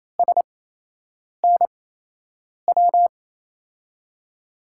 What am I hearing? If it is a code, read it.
Morse code, heard as a keyed tone, about 28 wpm, spaced as for 8 wpm, 700 Hz. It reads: SNW
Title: USA Call Signs - 5 Characters - 8x Character Spacing - 28wpm